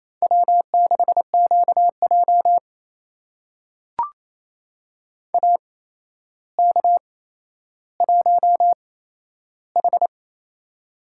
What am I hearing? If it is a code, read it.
Morse code, heard as a keyed tone, about 28 wpm, spaced as for 12 wpm, 700 Hz. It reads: W6QJ AK1H
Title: USA Call Signs - 8x Character Spacing - 28wpm